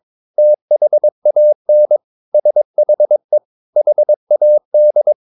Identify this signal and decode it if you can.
Morse code, heard as a keyed tone, about 22 wpm, 600 Hz. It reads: THAN SHE HAD
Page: Romanen Never af Ken Follet på engelsk.